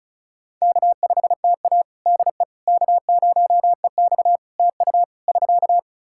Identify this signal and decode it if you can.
Morse code, heard as a keyed tone, about 36 wpm, 700 Hz. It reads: K5TA DE K0EX TU <SK>
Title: Practice QSO - Exchange 4 - Ending - 35wpm